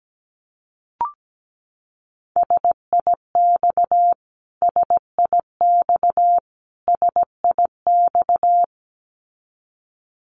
Morse code, heard as a keyed tone, about 17 wpm, 700 Hz. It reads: SIX SIX SIX
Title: Top 500 Words Repeated 17wpm